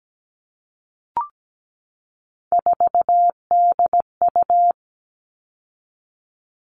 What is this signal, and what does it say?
Morse code, heard as a keyed tone, about 17 wpm, 700 Hz. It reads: 4DU
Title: All Cut Numbers - 3-Digit 17wpm